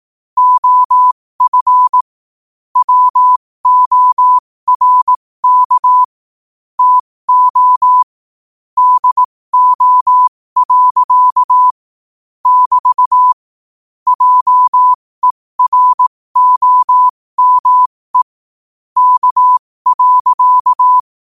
Morse code, heard as a keyed tone, about 18 wpm, spaced as for 15 wpm, 1000 Hz. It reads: OF WORK TO DO. <BT> JEROME K.
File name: qotdfiles / 15wpm